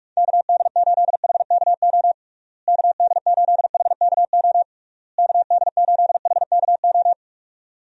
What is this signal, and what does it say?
Morse code, heard as a keyed tone, about 41 wpm, 700 Hz. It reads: KD8HKO KD8HKO KD8HKO
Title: USA Call Signs - 6 Characters - Repeated 3x - 45wpm